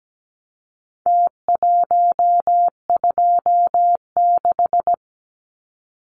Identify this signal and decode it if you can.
Morse code, heard as a keyed tone, about 17 wpm, 700 Hz. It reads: T126
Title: Numbers - Contest Style - 4-Digit 17wpm